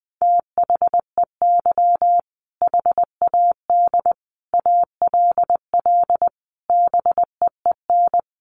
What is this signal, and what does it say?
Morse code, heard as a keyed tone, about 20 wpm, 700 Hz. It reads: THEY HAD ALL BEEN
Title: Sentences from Top 200 Words 20wpm